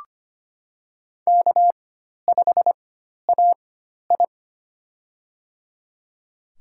Morse code, heard as a keyed tone, about 25 wpm, spaced as for 12 wpm, 700 Hz. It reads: K5AI
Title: USA Call Signs - 4 Characters - 4x Character Spacing - 25wpm